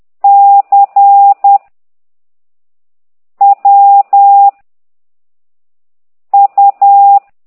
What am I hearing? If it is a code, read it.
Morse code, heard as a keyed tone, about 10 wpm, 800 Hz. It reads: C W U
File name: ct2idw / exerc9